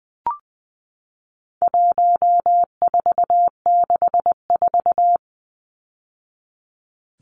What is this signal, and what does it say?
Morse code, heard as a keyed tone, about 20 wpm, 700 Hz. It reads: 1464
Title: Numbers - 4-Digit 20wpm